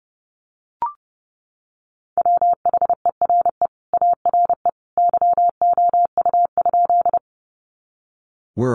Morse code, heard as a keyed tone, about 30 wpm, 700 Hz. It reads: WHERE ARE YOU?
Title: Sentences from Top 200 Words 30wpm